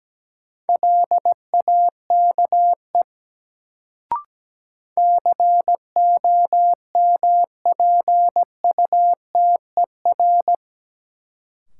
Morse code, heard as a keyed tone, about 17 wpm, 700 Hz. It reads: LAKE COMPUTER
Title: Sentences from Top 1000 Words - Review Part 3 - New Word Drill - 17wpm